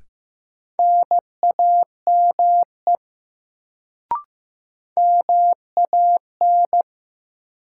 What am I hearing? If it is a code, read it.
Morse code, heard as a keyed tone, about 15 wpm, 700 Hz. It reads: NAME MAN
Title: Sentences from Top 200 Words - New Word Drill - 15wpm